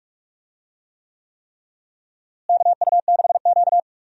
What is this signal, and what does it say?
Morse code, heard as a keyed tone, about 46 wpm, 700 Hz. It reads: KU6Q
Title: USA Call Signs - 4 Characters - Repeated 3x - 45wpm